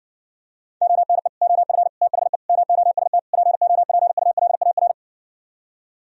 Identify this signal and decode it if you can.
Morse code, heard as a keyed tone, about 69 wpm, 700 Hz. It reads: ONE OF THE MOST POPULAR